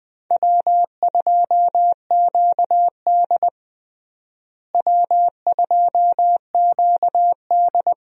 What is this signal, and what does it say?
Morse code, heard as a keyed tone, about 20 wpm, 700 Hz. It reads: W2QD W2QD
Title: USA Call Signs - 4 Characters - Repeated 3x - 20wpm